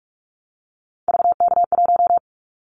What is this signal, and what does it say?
Morse code, heard as a keyed tone, about 46 wpm, 700 Hz. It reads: VK1
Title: Callsign Prefixes for Top 50 DXCC - 45wpm